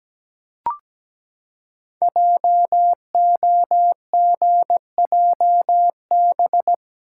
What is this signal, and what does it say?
Morse code, heard as a keyed tone, about 17 wpm, 700 Hz. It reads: JOGJB
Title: Code Groups - 5 Letters - 17wpm